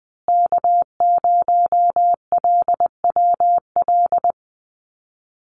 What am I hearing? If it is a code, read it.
Morse code, heard as a keyed tone, about 20 wpm, 700 Hz. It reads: K0LWL
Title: USA Call Signs Repeated 3x 20wpm